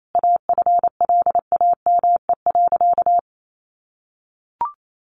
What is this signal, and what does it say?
Morse code, heard as a keyed tone, about 28 wpm, 700 Hz. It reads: AFLAME.
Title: A Sound Of Thunder 28wpm